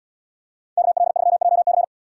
Morse code, heard as a keyed tone, about 73 wpm, 690 Hz. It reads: KL3JC